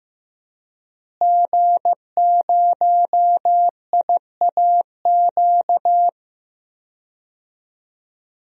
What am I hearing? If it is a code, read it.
Morse code, heard as a keyed tone, about 15 wpm, 700 Hz. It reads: G0IAQ
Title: DXCC-100 Call Signs - 5 Characters - Repeated 3x - 15wpm